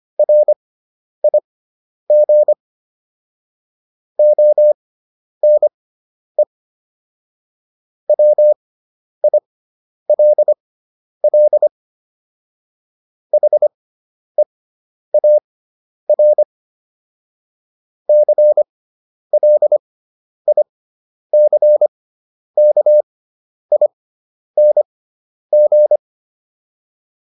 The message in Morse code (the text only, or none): RIG ONE WILL HEAR CLICKING